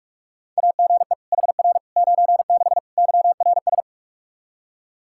Morse code, heard as a keyed tone, about 47 wpm, 700 Hz. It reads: AGE HR 96 YRS
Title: Practice QSO - Exchange 3 - More Info - 45wpm